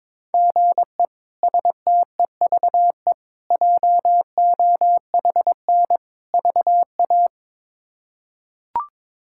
Morse code, heard as a keyed tone, about 22 wpm, 700 Hz. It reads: GE STEVE JOHN VA
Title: K1USN SST QSO Practice - 22wpm